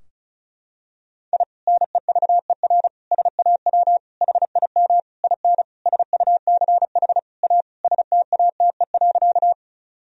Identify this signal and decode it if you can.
Morse code, heard as a keyed tone, about 36 wpm, 700 Hz. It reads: I NEVER SAW HIM IN SUCH A STATE.
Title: Sentences from Top 1000 Words - Review Part 2 - 35wpm